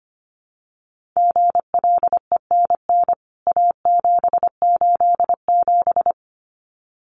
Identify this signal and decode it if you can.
Morse code, heard as a keyed tone, about 25 wpm, 700 Hz. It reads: GLENN A787
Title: CWT Exchange 25wpm